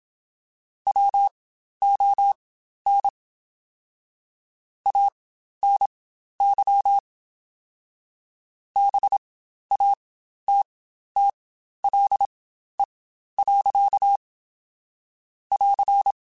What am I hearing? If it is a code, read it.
Morse code, heard as a keyed tone, about 27 wpm, spaced as for 12 wpm, 800 Hz. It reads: WON ANY BATTLE. <AR>
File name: Quotes Spaced / Quotes Spaced-mp3s / Pessimism sp @25